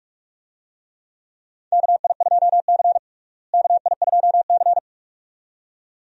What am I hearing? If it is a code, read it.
Morse code, heard as a keyed tone, about 46 wpm, 700 Hz. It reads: KI2C KI2C
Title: USA Call Signs - 4 Characters - Repeated 2x - 45wpm